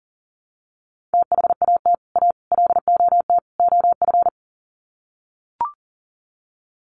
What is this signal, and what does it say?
Morse code, heard as a keyed tone, about 40 wpm, 700 Hz. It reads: THAT A LOT OF
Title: Sets of 4 Words 40wpm